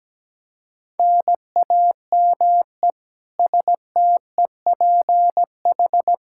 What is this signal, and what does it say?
Morse code, heard as a keyed tone, about 17 wpm, 700 Hz. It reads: NAME STEPH
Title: Practice QSO - Exchange 1 - Beginning - 17wpm